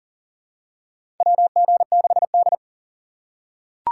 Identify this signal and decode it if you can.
Morse code, heard as a keyed tone, about 40 wpm, 700 Hz. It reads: WG6D
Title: USA Call Signs - 2x Character Spacing - 40wpm